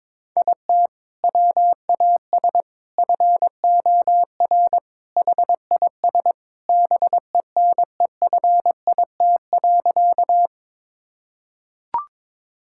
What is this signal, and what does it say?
Morse code, heard as a keyed tone, about 22 wpm, 700 Hz. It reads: IT WAS FOR HIS BENEFIT.
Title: Sentences from Top 1000 Words - Review Part 6 - 22wpm